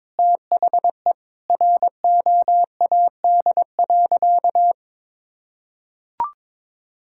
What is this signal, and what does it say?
Morse code, heard as a keyed tone, about 22 wpm, 700 Hz. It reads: THE ROAD.
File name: practice / 22wpm / Sentences with 4-Word Phrases - 22wpm